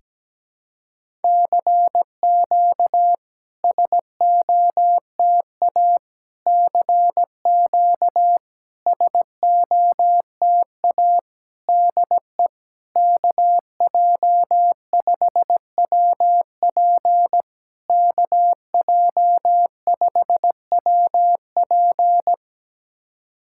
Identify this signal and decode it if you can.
Morse code, heard as a keyed tone, about 17 wpm, 700 Hz. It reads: CQ SOTA CQ SOTA DE KJ5WP KJ5WP
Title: SOTA QSO Practice 17wpm